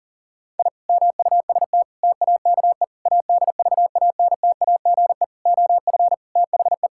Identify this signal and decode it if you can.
Morse code, heard as a keyed tone, about 40 wpm, 700 Hz. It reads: I MUST TAKE ADVANTAGE OF THE
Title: Sentences from Top 1000 Words - Review Part 6 - 40wpm